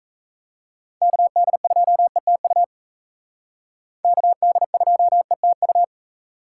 Morse code, heard as a keyed tone, about 36 wpm, 700 Hz. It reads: KD2ETU KD2ETU